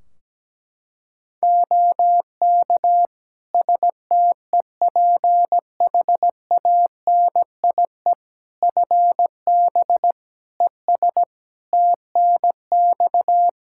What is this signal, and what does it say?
Morse code, heard as a keyed tone, about 17 wpm, 700 Hz. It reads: OK STEPHANIE FB ES TNX
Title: Practice QSO - Exchange 2 - Info - 17wpm